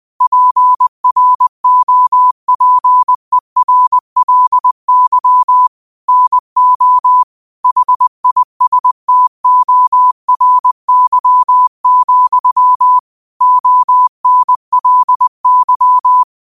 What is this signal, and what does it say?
Morse code, heard as a keyed tone, about 20 wpm, 1000 Hz. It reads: PROPERLY NO HISTORY, ONLY
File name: qotdfiles / 20wpm